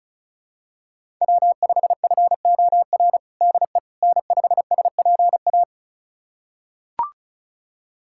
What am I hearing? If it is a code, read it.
Morse code, heard as a keyed tone, about 36 wpm, 690 Hz. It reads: W5FOR DE N5SPA
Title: USA Callsign DE Callsign - 5 Characters - 35wpm